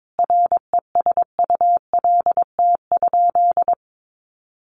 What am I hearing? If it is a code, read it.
Morse code, heard as a keyed tone, about 22 wpm, 700 Hz. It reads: RESULT?
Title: Sentences from Top 1500 Words 22wpm